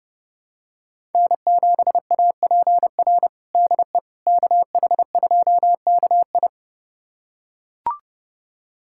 Morse code, heard as a keyed tone, about 32 wpm, 700 Hz. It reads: N7APR DE KH2KI
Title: USA Callsign DE Callsign - 5 Characters - 30wpm